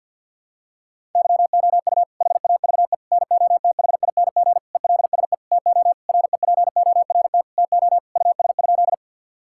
Morse code, heard as a keyed tone, about 52 wpm, 700 Hz. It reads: YOU HAVE NOTHING ELSE TO REPORT TO US?
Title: Sentences from Top 1000 Words - Review Part 6 - 50wpm